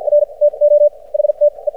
Morse code, heard as a keyed tone, about 54 wpm, spaced as for 38 wpm, 590 Hz. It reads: R TO HTE
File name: archived / CRSnet